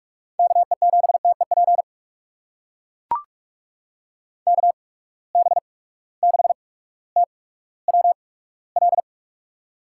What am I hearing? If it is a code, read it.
Morse code, heard as a keyed tone, about 43 wpm, spaced as for 19 wpm, 700 Hz. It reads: KE7TEP KB6TWL
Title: USA Call Signs - 6 Characters - 8x Character Spacing - 45wpm